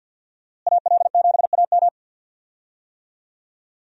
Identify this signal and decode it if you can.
Morse code, heard as a keyed tone, about 50 wpm, 700 Hz. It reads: AL7AM
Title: USA Call Signs - 5 Characters - Repeated 2x - 50wpm